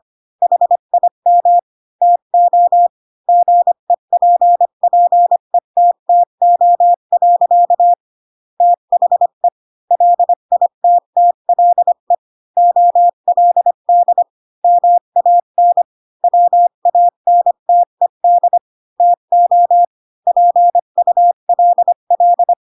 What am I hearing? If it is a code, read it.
Morse code, heard as a keyed tone, about 25 wpm, 700 Hz. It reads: HIM TO GEPPETTO. THE LITTLE OLD MAN WANTED TO PULL
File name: Pinocchio / Pinocchio ch3 20 25 ik1hge